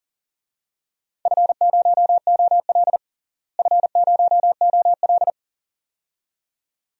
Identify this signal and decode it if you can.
Morse code, heard as a keyed tone, about 40 wpm, 700 Hz. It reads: F0OL F0OL
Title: DXCC-100 Call Signs - 4 Characters - Repeated 2x - 40wpm